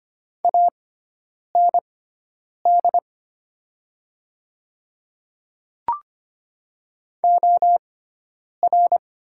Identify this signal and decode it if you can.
Morse code, heard as a keyed tone, about 25 wpm, spaced as for 9 wpm, 700 Hz. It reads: AND OR
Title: 6x Character Spacing - Top 300 Words - 25wpm